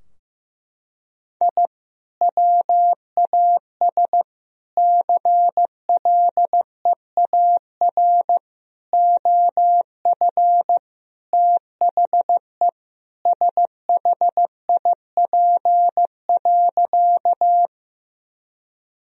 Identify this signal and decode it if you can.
Morse code, heard as a keyed tone, about 16 wpm, 700 Hz. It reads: I WAS CLEAR OF THE SHIP.
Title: Sentences from Top 1000 Words - Review Part 2 - 15wpm